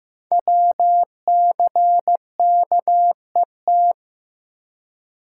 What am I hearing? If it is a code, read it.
Morse code, heard as a keyed tone, about 15 wpm, 700 Hz. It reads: WCKET
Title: Code Groups - 5 Letters - 15wpm